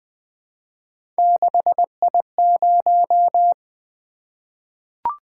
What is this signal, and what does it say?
Morse code, heard as a keyed tone, about 20 wpm, 700 Hz. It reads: 6I0
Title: Three Letter-Number Combinations - 20wpm